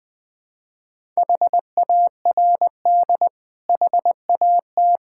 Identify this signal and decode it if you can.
Morse code, heard as a keyed tone, about 20 wpm, 700 Hz. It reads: HARD HAT
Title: Difficult Words with Hints - 20wpm